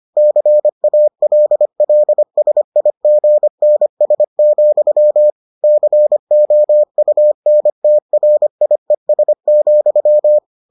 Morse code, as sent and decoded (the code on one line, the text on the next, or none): -.-. .- .-.. .-.. ... .. --. -. ... --..-- / -.-. --- ..- -. - .-. .. ... --..--
CALLSIGNS, COUNTRIES,